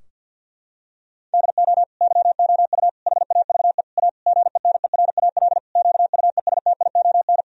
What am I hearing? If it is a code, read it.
Morse code, heard as a keyed tone, about 50 wpm, 700 Hz. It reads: DO YOU HAVE A GENERAL QUESTION
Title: Sentences from Top 1000 Words - Review Part 3 - 50wpm